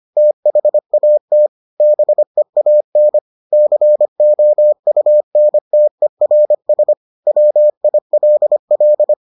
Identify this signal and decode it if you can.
Morse code, heard as a keyed tone, about 25 wpm, 600 Hz. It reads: THAT BEAN COUNTERS WILL